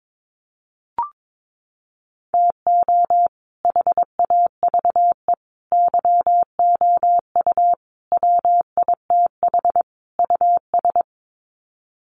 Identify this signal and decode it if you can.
Morse code, heard as a keyed tone, about 22 wpm, 700 Hz. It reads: TO HAVE YOU WITH US
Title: Sets of 5 Words 22wpm